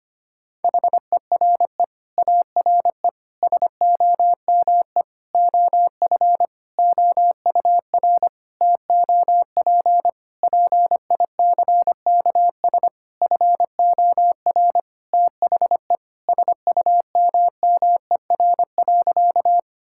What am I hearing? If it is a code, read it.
Morse code, heard as a keyed tone, about 25 wpm, 700 Hz. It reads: HERE ARE SOME OF OUR TOP PICKS FOR THE SUMMER.
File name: practice / 25wpm / Sentences with 9-Word Phrases - 25wpm